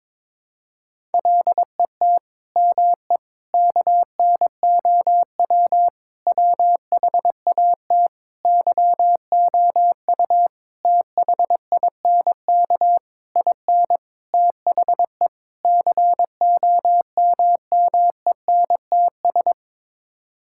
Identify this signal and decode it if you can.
Morse code, heard as a keyed tone, about 22 wpm, 700 Hz. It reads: LET ME KNOW WHAT YOU THINK IN THE COMMENTS
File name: Sets of 9 Words - 22wpm